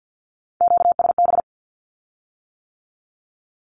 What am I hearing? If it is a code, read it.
Morse code, heard as a keyed tone, about 50 wpm, 700 Hz. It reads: QSB
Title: Q-Codes with Spoken Meaning - 50wpm